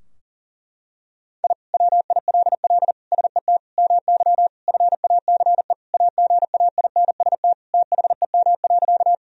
Morse code, heard as a keyed tone, about 40 wpm, 700 Hz. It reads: I WILL SET MY FACE AGAINST THEM.
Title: Sentences from Top 1000 Words - Review Part 2 - 40wpm